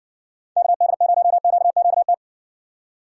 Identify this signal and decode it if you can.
Morse code, heard as a keyed tone, about 60 wpm, 700 Hz. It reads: KD0QQT